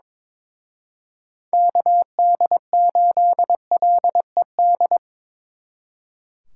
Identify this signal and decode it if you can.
Morse code, heard as a keyed tone, about 25 wpm, 700 Hz. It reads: KD8LED
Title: Speed Racing - USA 2x3 Call Signs 22wpm